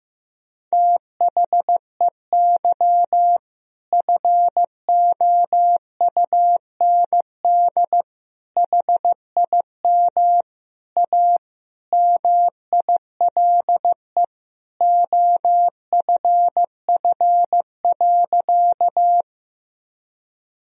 Morse code, heard as a keyed tone, about 16 wpm, 700 Hz. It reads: THEY FOUND HIM A MILE OFF.
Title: Sentences from Top 1000 Words - Review Part 1 - 15wpm